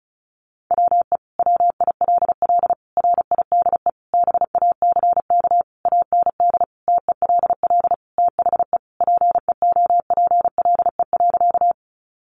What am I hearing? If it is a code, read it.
Morse code, heard as a keyed tone, about 34 wpm, 700 Hz. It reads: WE WILL RIDE BACK AND TELL THE PEOPLE.
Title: Sentences from Top 1200 Words 35wpm